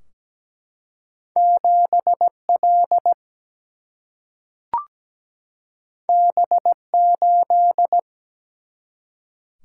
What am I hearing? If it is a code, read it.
Morse code, heard as a keyed tone, about 17 wpm, 700 Hz. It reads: 7 L B 8
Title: Single-Letter Single-Number 17wpm